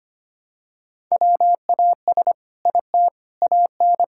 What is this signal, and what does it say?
Morse code, heard as a keyed tone, about 25 wpm, 700 Hz. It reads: WAS IT AN
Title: Sentences from Top 1000 Words - Review Part 5 - 25wpm